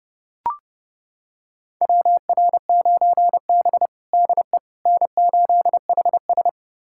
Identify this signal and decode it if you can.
Morse code, heard as a keyed tone, about 30 wpm, 700 Hz. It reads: WR9B DE N8HS
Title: USA Callsign DE Callsign - 4 Characters - 30wpm